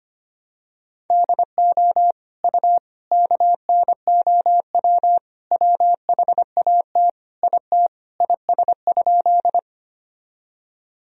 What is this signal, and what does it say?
Morse code, heard as a keyed tone, about 25 wpm, 700 Hz. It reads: DO U KNOW WHAT IT IS?
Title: Sentences from Top 100 Words and Abbreviations 25wpm